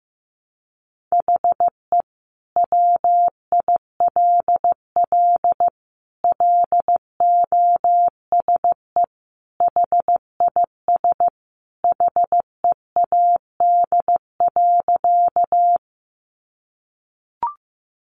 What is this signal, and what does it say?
Morse code, heard as a keyed tone, about 15 wpm, 700 Hz. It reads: HE WILL LOSE HIS HEAD.
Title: Sentences from Top 1100 Words 15wpm